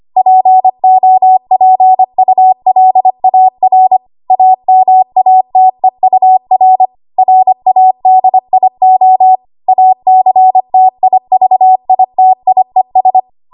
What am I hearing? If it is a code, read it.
Morse code, heard as a keyed tone, about 25 wpm, 750 Hz. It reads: POPULAR AMATEUR RADIO ACTIVITIES
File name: Archive / 25 WPM / 25WPM